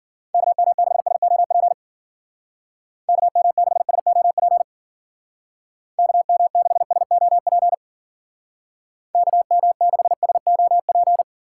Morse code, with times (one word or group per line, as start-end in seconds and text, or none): 0.3-1.7 KM6SOP
3.1-4.6 KM6SOP
6.0-7.7 KM6SOP
9.1-11.2 KM6SOP